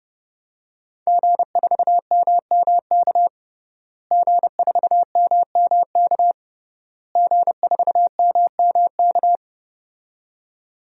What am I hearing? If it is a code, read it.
Morse code, heard as a keyed tone, about 30 wpm, 700 Hz. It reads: G4MMK G4MMK G4MMK
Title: DXCC-100 Call Signs - 5 Characters - Repeated 3x - 30wpm